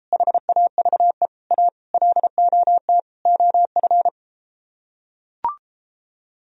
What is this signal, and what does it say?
Morse code, heard as a keyed tone, about 33 wpm, 700 Hz. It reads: HAVE A LOT OF